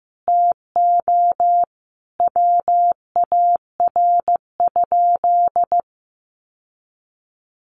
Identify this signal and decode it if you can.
Morse code, heard as a keyed tone, about 15 wpm, 700 Hz. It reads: TO WAR?
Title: Sentences from Top 1400 Words 15wpm